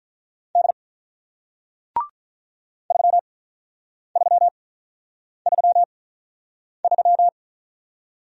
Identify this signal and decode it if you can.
Morse code, heard as a keyed tone, about 48 wpm, spaced as for 9 wpm, 700 Hz. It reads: D 3333